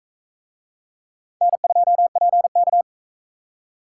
Morse code, heard as a keyed tone, about 41 wpm, 700 Hz. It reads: N2PK